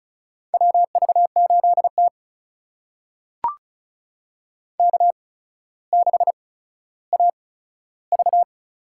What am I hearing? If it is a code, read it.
Morse code, heard as a keyed tone, about 36 wpm, spaced as for 15 wpm, 700 Hz. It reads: WV8T K6AV
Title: USA Call Signs - 4 Characters - 8x Character Spacing - 35wpm